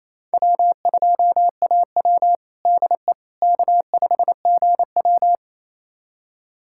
Morse code, heard as a keyed tone, about 28 wpm, 700 Hz. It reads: W2AW DE K5GW
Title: USA Callsign DE Callsign - 4 Characters - 28wpm